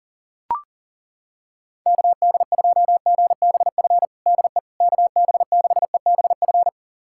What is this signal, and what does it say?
Morse code, heard as a keyed tone, about 40 wpm, 700 Hz. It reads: KD2GBF DE KB6EBF
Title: USA Callsign DE Callsign 40wpm